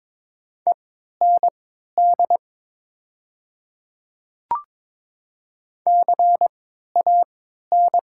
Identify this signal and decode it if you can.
Morse code, heard as a keyed tone, about 21 wpm, spaced as for 12 wpm, 700 Hz. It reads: END CAN
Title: 3x Character Spacing - Top 200 Words - 22wpm